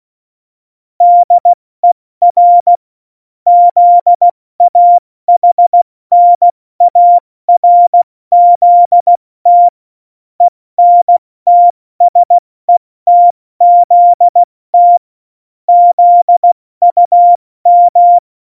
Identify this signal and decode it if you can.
Morse code, heard as a keyed tone, about 16 wpm, 700 Hz. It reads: DER ZAHNARZT ENTSETZT ZUM
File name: Morsekurs / Witze / Witz 05. Dezember 2017, 70 F80